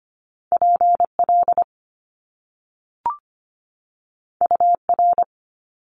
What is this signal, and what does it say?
Morse code, heard as a keyed tone, about 25 wpm, 700 Hz. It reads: PL UR
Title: Top Two Letter Combinations in Words 25wpm